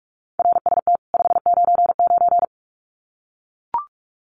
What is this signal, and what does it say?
Morse code, heard as a keyed tone, about 44 wpm, 700 Hz. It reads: RST 599
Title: Common QSO Phrases 45wpm